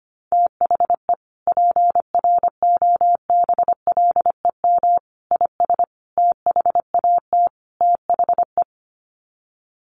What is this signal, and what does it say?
Morse code, heard as a keyed tone, about 25 wpm, 700 Hz. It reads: THE PROBLEM IS THAT THE
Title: Sets of 5 Words Encore2 25wpm